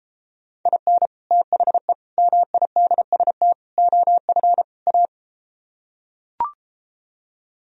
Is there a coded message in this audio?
Yes: Morse code from 0.7 to 5.1 s, heard as a keyed tone, about 33 wpm, 700 Hz. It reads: IN THE MIDST OF A